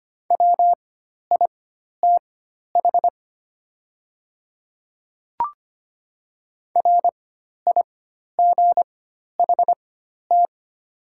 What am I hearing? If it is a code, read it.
Morse code, heard as a keyed tone, about 25 wpm, spaced as for 12 wpm, 700 Hz. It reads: WITH RIGHT
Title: 4x Character Spacing - Top 300 Words - 25wpm